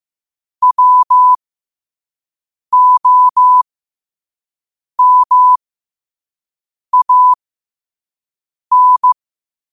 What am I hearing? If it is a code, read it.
Morse code, heard as a keyed tone, about 15 wpm, spaced as for 5 wpm, 1000 Hz. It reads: WOMAN
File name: qotdfiles / 5wpm